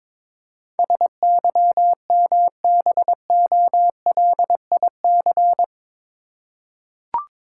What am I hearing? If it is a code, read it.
Morse code, heard as a keyed tone, about 22 wpm, 700 Hz. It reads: SYMBOLIC
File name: practice / 22wpm / Eight Letter Words - 22wpm